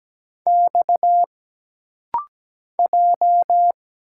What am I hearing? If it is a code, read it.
Morse code, heard as a keyed tone, about 17 wpm, 700 Hz. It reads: X J
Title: Single Letters - Rapid-Fire 17wpm